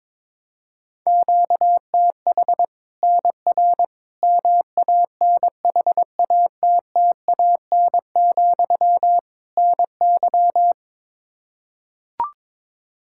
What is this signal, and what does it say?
Morse code, heard as a keyed tone, about 22 wpm, 700 Hz. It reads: QTH NR MANHATTAN, NY
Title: Practice QSO - Exchange 1 - Beginning - 22wpm